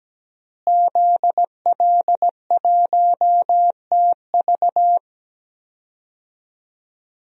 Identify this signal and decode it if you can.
Morse code, heard as a keyed tone, about 17 wpm, 700 Hz. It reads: ZL1TV
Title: DXCC-100 Call Signs - 5 Characters - Repeated 3x - 17wpm